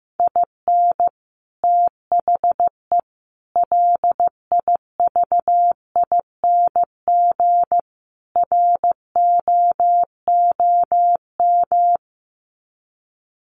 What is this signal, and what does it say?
Morse code, heard as a keyed tone, about 15 wpm, 700 Hz. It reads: IN THE LIVING ROOM
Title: Sets of 4 Words 15wpm